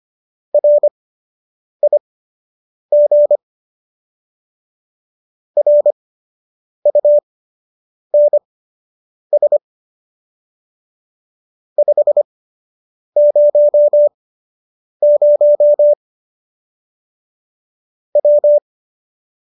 Morse code, heard as a keyed tone, about 25 wpm, spaced as for 8 wpm, 590 Hz. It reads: RIG RUNS 500 W